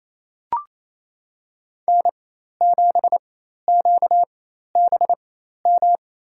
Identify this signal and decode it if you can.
Morse code, heard as a keyed tone, about 28 wpm, spaced as for 13 wpm, 700 Hz. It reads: N7QBM
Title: USA Call Signs - 5 Characters - 4x Character Spacing - 28wpm